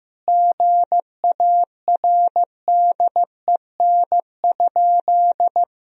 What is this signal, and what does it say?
Morse code, heard as a keyed tone, about 15 wpm, 700 Hz. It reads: GARDEN?
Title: Sentences from Top 1000 Words - Review Part 4 - 15wpm